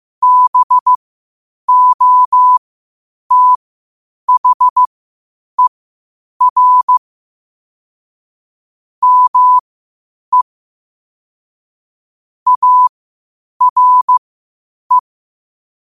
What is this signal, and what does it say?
Morse code, heard as a keyed tone, about 15 wpm, spaced as for 8 wpm, 1000 Hz. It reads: BOTHER ME ARE
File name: qotdfiles / 8wpm